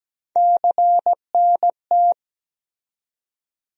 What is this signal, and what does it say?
Morse code, heard as a keyed tone, about 17 wpm, 700 Hz. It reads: CNT
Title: USA Call Sign Postfixes - 3 Letters 17wpm